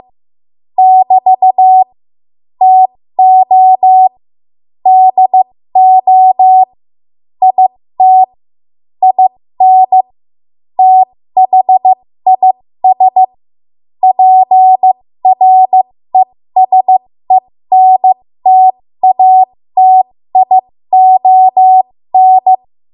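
Morse code, heard as a keyed tone, about 15 wpm, spaced as for 13 wpm, 750 Hz. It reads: <BT> TO DO IT IN THIS PRESENTATION
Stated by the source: 13 wpm Morse code practice archive transmission.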